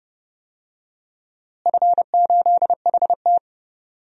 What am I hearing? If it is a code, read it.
Morse code, heard as a keyed tone, about 30 wpm, 700 Hz. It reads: F8HT
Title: DXCC-100 Call Signs - 4 Characters - Repeated 2x - 30wpm